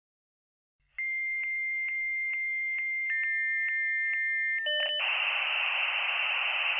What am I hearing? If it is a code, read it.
Dial-up modem handshake.